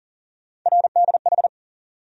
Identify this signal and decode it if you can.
Morse code, heard as a keyed tone, about 40 wpm, 700 Hz. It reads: RDH
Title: USA Call Sign Postfixes - 3 Letters 40wpm